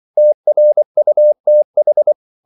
Morse code, heard as a keyed tone, about 24 wpm, 600 Hz. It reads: TRUTH
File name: CW / Practice / If24